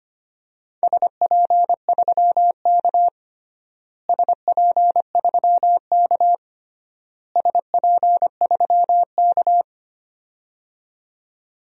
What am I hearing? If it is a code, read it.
Morse code, heard as a keyed tone, about 25 wpm, 700 Hz. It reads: SP3K SP3K SP3K
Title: DXCC-100 Call Signs - 4 Characters - Repeated 3x - 25wpm